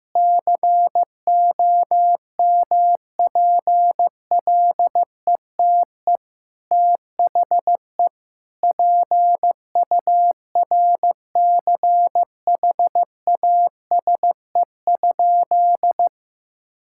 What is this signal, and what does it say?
Morse code, heard as a keyed tone, about 15 wpm, 700 Hz. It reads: COMPLETE THE PURCHASE?
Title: Sentences from Top 1000 Words - Review Part 5 - 15wpm